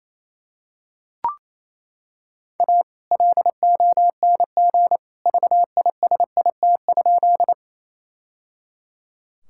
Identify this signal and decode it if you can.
Morse code, heard as a keyed tone, about 28 wpm, 700 Hz. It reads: A LONG VISIT?
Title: Sentences from Top 1000 Words - Review Part 6 - 28wpm